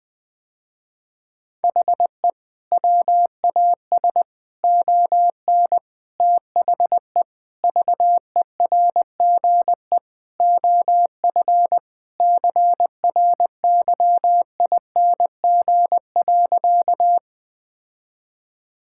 Morse code, heard as a keyed tone, about 20 wpm, 700 Hz. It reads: HE WAS ON THE VERGE OF CRYING.
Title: Sentences with 4-Word Phrases - Spoken Beginning - 20wpm